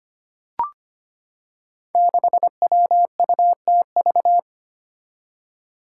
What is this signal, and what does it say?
Morse code, heard as a keyed tone, about 25 wpm, 700 Hz. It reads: 6WUTV
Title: Five Challenging Letter-Number Combinations 25wpm